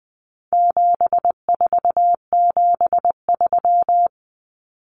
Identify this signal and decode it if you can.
Morse code, heard as a keyed tone, about 20 wpm, 700 Hz. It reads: 7473
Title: Common Cut Numbers - 4-Digit 20wpm